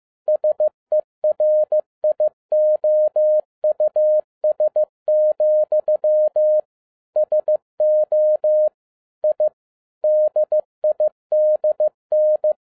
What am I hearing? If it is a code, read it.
Morse code, heard as a keyed tone, about 15 wpm, 610 Hz. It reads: SERIOUS, SO I DIDN
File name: 2011-07-mp3 / n57 BACK FROM A BIG ACCIDENT 15wpm